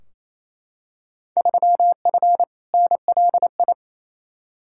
Morse code, heard as a keyed tone, about 28 wpm, 700 Hz. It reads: 3F NLI
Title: Field Day Exchanges Encore - 28wpm